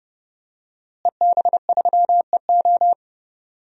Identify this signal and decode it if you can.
Morse code, heard as a keyed tone, about 30 wpm, 700 Hz. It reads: EB3EO
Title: DXCC-100 Call Signs - 5 Characters - Repeated 2x - 30wpm